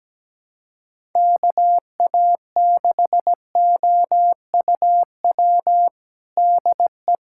Morse code, heard as a keyed tone, about 17 wpm, 700 Hz. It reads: KA6OUW DE
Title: USA Callsign DE Callsign - 6 Characters - 17wpm